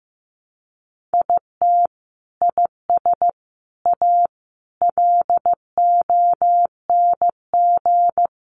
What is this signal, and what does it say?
Morse code, heard as a keyed tone, about 15 wpm, 700 Hz. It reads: IT IS A LONG
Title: Sentences from Top 300 Words 15wpm